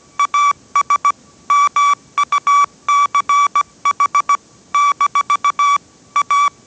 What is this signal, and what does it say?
Morse code, heard as a keyed tone, about 17 wpm, 1150 Hz. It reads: AS MUCH - A
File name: 18wpm